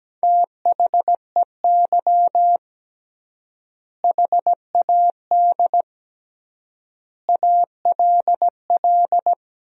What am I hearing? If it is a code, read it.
Morse code, heard as a keyed tone, about 17 wpm, 700 Hz. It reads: THEY HAD ALL
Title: 3x Word-Spacing - Sentences from Top 200 Words - 17wpm